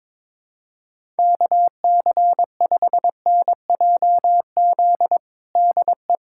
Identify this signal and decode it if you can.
Morse code, heard as a keyed tone, about 22 wpm, 700 Hz. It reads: KC5NJZ DE
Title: USA Callsign DE Callsign 22wpm